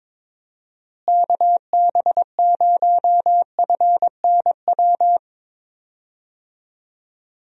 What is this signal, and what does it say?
Morse code, heard as a keyed tone, about 21 wpm, 700 Hz. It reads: KB0FNW
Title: USA Call Signs - 6 Characters - Repeated 2x - 22wpm